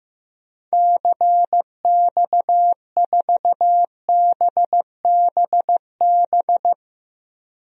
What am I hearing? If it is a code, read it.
Morse code, heard as a keyed tone, about 15 wpm, 700 Hz. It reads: CX4BBB
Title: DXCC-100 Call Signs - 6 Characters - 15wpm